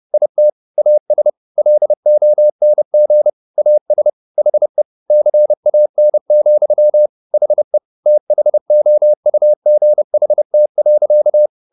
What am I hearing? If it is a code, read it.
Morse code, heard as a keyed tone, about 30 wpm, 600 Hz. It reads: IT AS LONG AS HE CAN, HE THOUGHT.
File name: the old man and the sea / the old man and the sea 1 30wpm